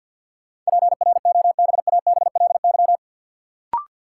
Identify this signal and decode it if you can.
Morse code, heard as a keyed tone, about 52 wpm, 700 Hz. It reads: PROBABLY
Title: Sentences from Top 700 Words - New Word Drill - 50wpm